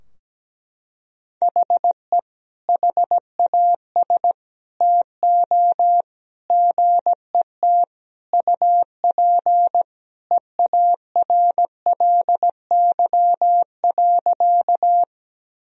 Morse code, heard as a keyed tone, about 17 wpm, 700 Hz. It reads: HE HAS TO GET UP EARLY.
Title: Sentences from Top 1000 Words - Review Part 2 - 17wpm